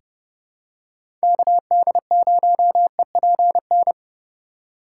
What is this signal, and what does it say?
Morse code, heard as a keyed tone, about 32 wpm, 700 Hz. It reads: KD0EPN
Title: USA Call Signs - 6 Characters - 3x Character Spacing - 30wpm